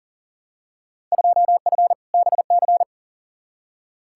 Morse code, heard as a keyed tone, about 40 wpm, 700 Hz. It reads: 2F BC
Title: Field Day Exchanges Encore - 40wpm